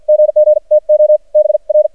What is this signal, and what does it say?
Morse code, heard as a keyed tone, about 53 wpm, spaced as for 44 wpm, 610 Hz. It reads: GG TO BR
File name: archived / CRSnet